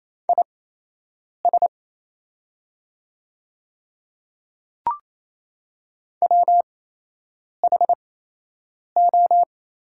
Morse code, heard as a keyed tone, about 28 wpm, spaced as for 8 wpm, 700 Hz. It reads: IS WHO
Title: 8x Character Spacing - Top 100 Words - 28wpm